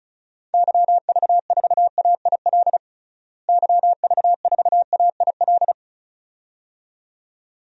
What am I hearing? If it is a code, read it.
Morse code, heard as a keyed tone, about 36 wpm, 700 Hz. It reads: YV4AIL YV4AIL
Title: DXCC-100 Call Signs - 6 Characters - Repeated 2x - 35wpm